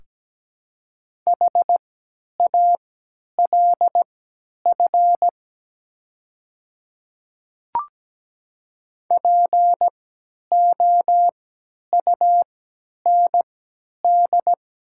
Morse code, heard as a keyed tone, about 17 wpm, spaced as for 10 wpm, 700 Hz. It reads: HALF POUND
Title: 3x Character Spacing - Top 500 Words - 17wpm